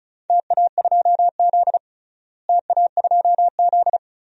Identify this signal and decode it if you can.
Morse code, heard as a keyed tone, about 34 wpm, 700 Hz. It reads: TA2Z TA2Z
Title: DXCC-100 Call Signs - 4 Characters - Repeated 3x - 35wpm